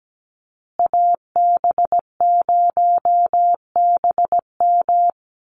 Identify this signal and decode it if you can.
Morse code, heard as a keyed tone, about 17 wpm, 700 Hz. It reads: AB0BM
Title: USA Call Signs Repeated 2x 17wpm